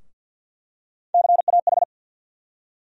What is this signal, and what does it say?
Morse code, heard as a keyed tone, about 49 wpm, 700 Hz. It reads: CAH
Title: USA Call Sign Postfixes - 3 Letters 50wpm